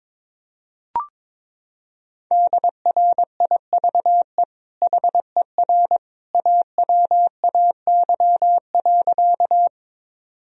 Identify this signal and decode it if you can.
Morse code, heard as a keyed tone, about 21 wpm, 700 Hz. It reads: DRIVE HER AWAY.
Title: Sentences from Top 1000 Words - Review Part 2 - 22wpm